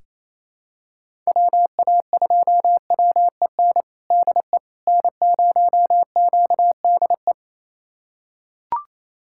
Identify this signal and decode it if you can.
Morse code, heard as a keyed tone, about 28 wpm, 700 Hz. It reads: WA2WEN DE N0QDE
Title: USA Callsign DE Callsign 28wpm